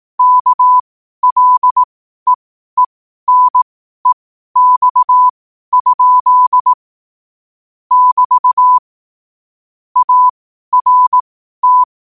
Morse code, heard as a keyed tone, about 18 wpm, spaced as for 12 wpm, 1000 Hz. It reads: KLEENEX? <BT> ART